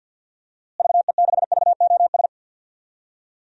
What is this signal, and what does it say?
Morse code, heard as a keyed tone, about 53 wpm, 700 Hz. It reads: VE6VOS
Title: DXCC-100 Call Signs - 6 Characters - Repeated 2x - 50wpm